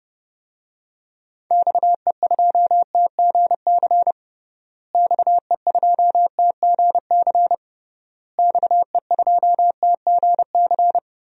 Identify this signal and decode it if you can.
Morse code, heard as a keyed tone, about 30 wpm, 700 Hz. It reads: XE2TGC XE2TGC XE2TGC
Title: DXCC-100 Call Signs - 6 Characters - Repeated 3x - 30wpm